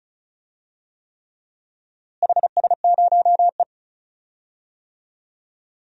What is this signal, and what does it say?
Morse code, heard as a keyed tone, about 36 wpm, 700 Hz. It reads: HS0E
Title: DXCC-100 Call Signs - 4 Characters - Repeated 3x - 35wpm